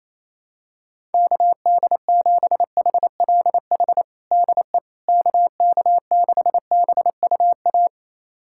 Morse code, heard as a keyed tone, about 28 wpm, 700 Hz. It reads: KD7HLH DE KK6BUA
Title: USA Callsign DE Callsign - 6 Characters - 28wpm